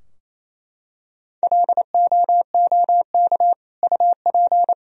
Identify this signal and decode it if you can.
Morse code, heard as a keyed tone, about 28 wpm, 700 Hz. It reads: LOOK UP
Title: Sentences from Top 1000 Words - Review Part 4 - 28wpm